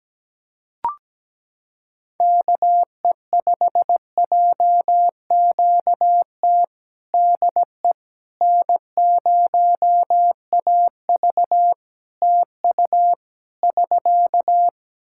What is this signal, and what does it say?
Morse code, heard as a keyed tone, about 17 wpm, 700 Hz. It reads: KE5JQT DE N0AV TU <SK>
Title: Practice QSO - Exchange 4 - Ending - 17wpm